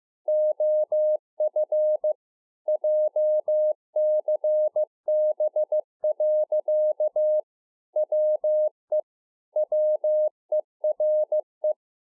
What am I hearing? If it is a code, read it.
Morse code, heard as a keyed tone, about 15 wpm, 610 Hz. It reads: OF JCB. WE WERE